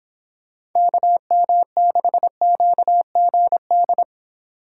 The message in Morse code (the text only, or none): KM6QGD